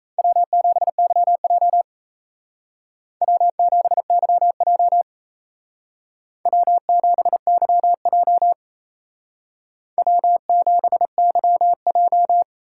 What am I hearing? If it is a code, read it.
Morse code, heard as a keyed tone, about 36 wpm, 700 Hz. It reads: W7YJ W7YJ W7YJ W7YJ